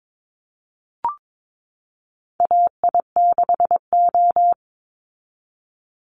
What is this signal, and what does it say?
Morse code, heard as a keyed tone, about 22 wpm, 690 Hz. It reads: AI6O
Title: Call Signs Repeated 3x - 22wpm